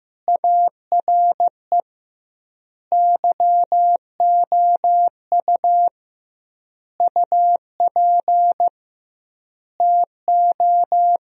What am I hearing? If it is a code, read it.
Morse code, heard as a keyed tone, about 16 wpm, 700 Hz. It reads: ARE YOU UP TO
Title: 2x Word-Spacing - Sentences from Top 100 Words - 15wpm